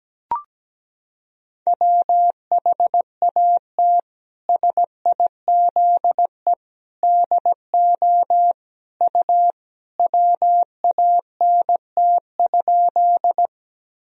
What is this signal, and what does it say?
Morse code, heard as a keyed tone, about 17 wpm, 700 Hz. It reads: WHAT SIZE DO U WANT?
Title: Sentences from Top 400 Words and Abbreviations 17wpm